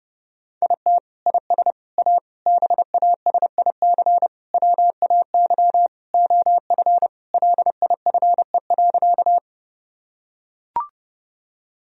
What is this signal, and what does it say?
Morse code, heard as a keyed tone, about 30 wpm, 700 Hz. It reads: IT IS A BASIC WAY OF LIFE.
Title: Sentences from Top 1000 Words - Review Part 3 - 30wpm